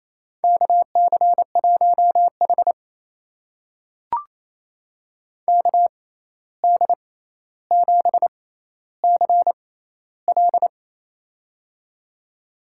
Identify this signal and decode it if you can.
Morse code, heard as a keyed tone, about 28 wpm, spaced as for 13 wpm, 700 Hz. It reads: KC1H KD7CL
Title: USA Call Signs - 6x Character Spacing - 28wpm